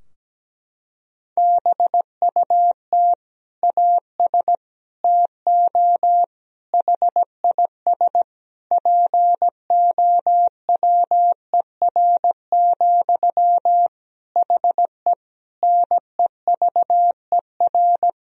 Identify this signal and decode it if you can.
Morse code, heard as a keyed tone, about 17 wpm, 700 Hz. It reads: BUT AS TO HIS POWER, HE NEVER
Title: Sentences from Top 1000 Words - Review Part 2 - 17wpm